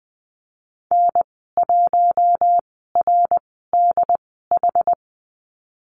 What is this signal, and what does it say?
Morse code, heard as a keyed tone, about 20 wpm, spaced as for 14 wpm, 700 Hz. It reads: N1RDH
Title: 154 Lesson 25g - Letter p - Call Signs 20wpm